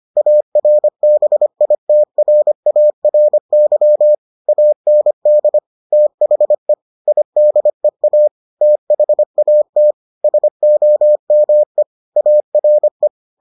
Morse code, heard as a keyed tone, about 25 wpm, 600 Hz. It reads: ARBITRARY AND THE IDEA THAT SOME ARE